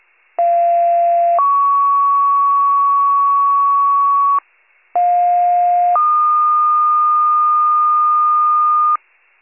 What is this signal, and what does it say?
Two-tone page, heard as a one-second tone, then a three-second tone, about 690 then 1120 Hz, about 710 then 1220 Hz. It is heard from 0.4 to 9.0 s.